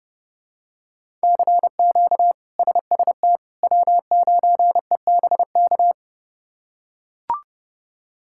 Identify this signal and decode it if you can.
Morse code, heard as a keyed tone, about 30 wpm, 700 Hz. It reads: CQ SST W9EBK
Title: K1USN SST QSO Practice - 30wpm